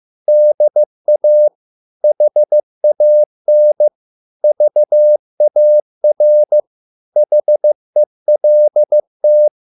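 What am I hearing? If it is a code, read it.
Morse code, heard as a keyed tone, about 15 wpm, 600 Hz. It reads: DA HAN VAR HELT
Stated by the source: Morse code at a slow 15 wpm from CW med Gnister.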